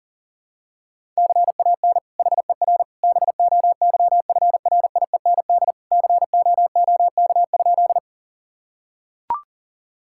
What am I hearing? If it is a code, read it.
Morse code, heard as a keyed tone, about 40 wpm, 700 Hz. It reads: CAN HER BOYFRIEND COOK?
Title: Sentences from Top 1000 Words - Review Part 5 - 40wpm